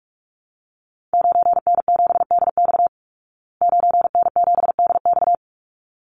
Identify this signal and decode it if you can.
Morse code, heard as a keyed tone, about 44 wpm, 700 Hz. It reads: 9N7DX 9N7DX
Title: Call Signs Repeated 3x - 45wpm